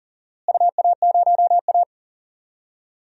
Morse code, heard as a keyed tone, about 37 wpm, 700 Hz. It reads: UA0A
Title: DXCC-100 Call Signs - 4 Characters - Repeated 3x - 40wpm